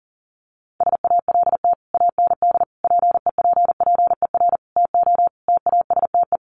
Morse code, heard as a keyed tone, about 40 wpm, 700 Hz. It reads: SALT AND PEPPER TO TASTE
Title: Sets of 5 Words 40wpm